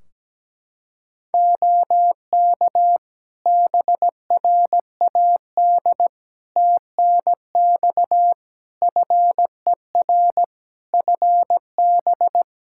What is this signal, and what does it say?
Morse code, heard as a keyed tone, about 17 wpm, 700 Hz. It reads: OK BRAD TNX FER FB
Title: K1USN SST QSO Practice 17wpm